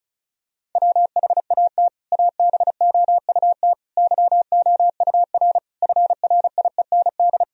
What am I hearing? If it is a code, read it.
Morse code, heard as a keyed tone, about 34 wpm, 700 Hz. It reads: WHAT ABOUT YOUR FRIEND
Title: Sentences from Top 1000 Words - Review Part 1 - 35wpm